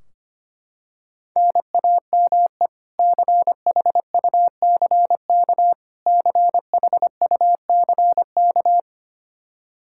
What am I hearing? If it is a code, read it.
Morse code, heard as a keyed tone, about 25 wpm, 700 Hz. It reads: NAME CHUCK CHUCK
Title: K1USN SST QSO Practice 25wpm